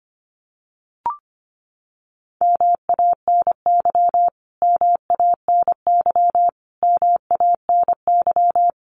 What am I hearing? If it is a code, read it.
Morse code, heard as a keyed tone, about 25 wpm, 700 Hz. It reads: MANY MANY MANY
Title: Top 200 Words Repeated 25wpm